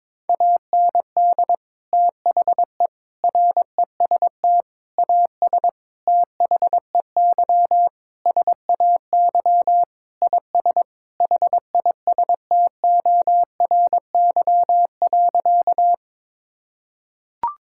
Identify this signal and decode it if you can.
Morse code, heard as a keyed tone, about 22 wpm, 700 Hz. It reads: AND THE REST AS THEY SAY IS HISTORY.
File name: practice / 22wpm / Sentences with 8-Word Phrases - 22wpm